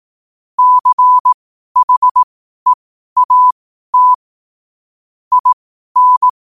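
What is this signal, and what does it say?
Morse code, heard as a keyed tone, about 18 wpm, spaced as for 12 wpm, 1000 Hz. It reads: CHEAT IN
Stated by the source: Morse code quote practice at 12 words per minute.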